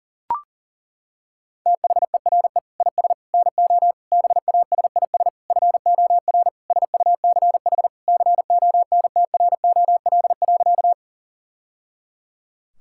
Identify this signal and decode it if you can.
Morse code, heard as a keyed tone, about 37 wpm, 700 Hz. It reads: THERE IS NO BASIS FOR SUCH CONTROL.
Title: Sentences from Top 1000 Words - Review Part 5 - 40wpm